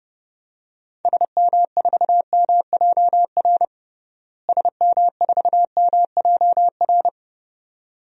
Morse code, heard as a keyed tone, about 29 wpm, 700 Hz. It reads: SM4MJR SM4MJR
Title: DXCC-100 Call Signs - 6 Characters - Repeated 2x - 30wpm